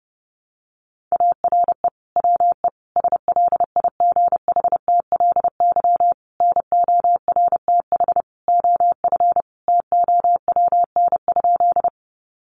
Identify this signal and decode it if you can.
Morse code, heard as a keyed tone, about 30 wpm, 700 Hz. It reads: ARE WE SLIGHTLY NORTH OF TOWN?
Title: Sentences from Top 1300 Words 30wpm